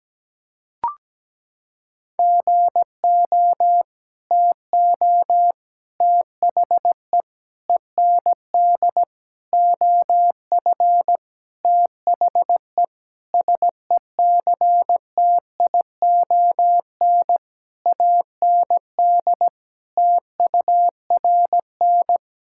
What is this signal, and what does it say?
Morse code, heard as a keyed tone, about 17 wpm, 700 Hz. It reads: GO TO THE END OF THE SECTION AND TURN
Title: Sentences from Top 1000 Words - Review Part 5 - 17wpm